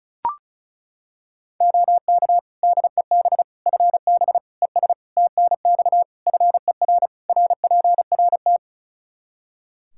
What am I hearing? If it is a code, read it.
Morse code, heard as a keyed tone, about 36 wpm, 700 Hz. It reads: OK DEB FB ES TNX FER RPRT
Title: Practice QSO - Exchange 2 - Info - 35wpm